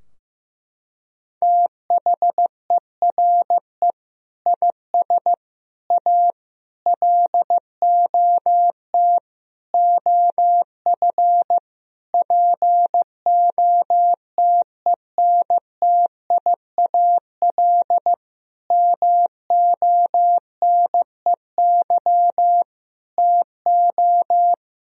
Morse code, heard as a keyed tone, about 15 wpm, 700 Hz. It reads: THERE IS A LOT OF POTENTIAL MONEY TO
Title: Sentences from Top 1000 Words - Review Part 4 - 15wpm